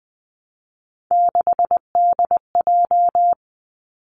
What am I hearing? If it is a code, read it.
Morse code, heard as a keyed tone, about 20 wpm, 700 Hz. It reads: 6DJ
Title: Three Challenging Letter-Number Combinations 20wpm